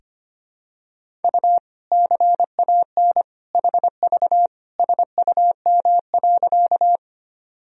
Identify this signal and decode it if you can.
Morse code, heard as a keyed tone, about 25 wpm, 700 Hz. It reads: U CAN HV SUM.
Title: Sentences from Top 100 Words and Abbreviations 25wpm